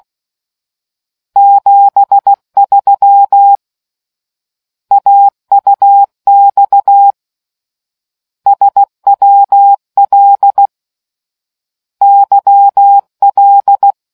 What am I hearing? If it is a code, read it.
Morse code, heard as a keyed tone, about 16 wpm, 800 Hz. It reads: 73 AUX SWL YL